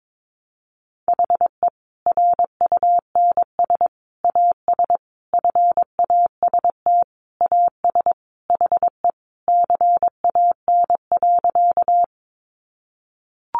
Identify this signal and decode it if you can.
Morse code, heard as a keyed tone, about 22 wpm, 700 Hz. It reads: HE RUNS AS FAST AS HE CAN.
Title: Additional Easy Phrases 22wpm